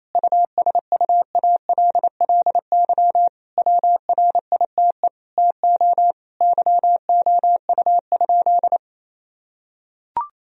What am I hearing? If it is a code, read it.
Morse code, heard as a keyed tone, about 27 wpm, 700 Hz. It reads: USUALLY WRITE TO YOU?
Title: Sentences from Top 1000 Words - Review Part 3 - 28wpm